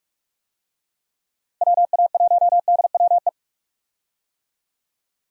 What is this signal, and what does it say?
Morse code, heard as a keyed tone, about 46 wpm, 700 Hz. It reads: WA1DWE
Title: USA Call Signs - 6 Characters - Repeated 3x - 45wpm